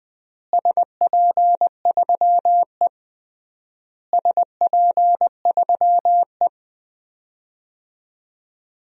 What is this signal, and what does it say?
Morse code, heard as a keyed tone, about 21 wpm, 700 Hz. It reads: SP3E SP3E
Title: DXCC-100 Call Signs - 4 Characters - Repeated 3x - 20wpm